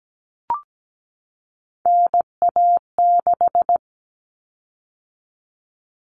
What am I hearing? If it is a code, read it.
Morse code, heard as a keyed tone, about 17 wpm, 700 Hz. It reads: NA6
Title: Common Cut Numbers - 3-Digit 17wpm